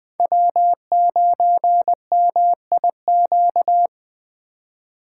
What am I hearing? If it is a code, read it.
Morse code, heard as a keyed tone, about 20 wpm, 700 Hz. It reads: W9MIQ
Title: USA Call Signs - 2x Character Spacing - 20wpm